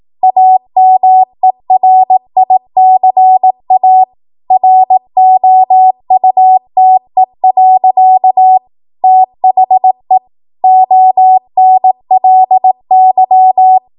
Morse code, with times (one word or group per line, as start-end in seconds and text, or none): 0.2-4.0 AMERICA
4.5-8.6 ROUTE.
9.0-10.2 THE
10.6-13.8 ONLY